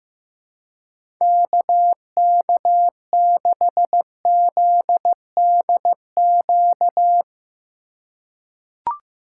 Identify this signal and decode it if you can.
Morse code, heard as a keyed tone, about 16 wpm, 700 Hz. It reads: KK6ZDQ
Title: USA Call Signs - 6 Characters - 4x Character Spacing - 15wpm